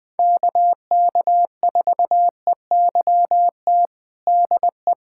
Morse code, heard as a keyed tone, about 20 wpm, 700 Hz. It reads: KK4EYT DE
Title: USA Callsign DE Callsign - 6 Characters - 20wpm